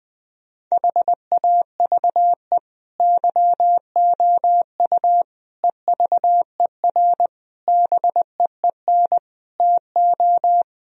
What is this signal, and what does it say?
Morse code, heard as a keyed tone, about 20 wpm, 700 Hz. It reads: HAVE YOU EVER BEEN TO
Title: Sentences from Top 1000 Words - Review Part 1 - 20wpm